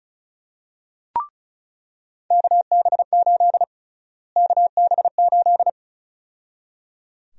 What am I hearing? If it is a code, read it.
Morse code, heard as a keyed tone, about 34 wpm, 700 Hz. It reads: KB8 KB8
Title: USA Call Sign Prefixes - Repeated 2x - 35wpm